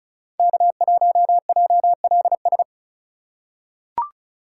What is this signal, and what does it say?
Morse code, heard as a keyed tone, about 35 wpm, 690 Hz. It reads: K1JLS
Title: USA Call Signs - 5 Characters - 2x Character Spacing - 35wpm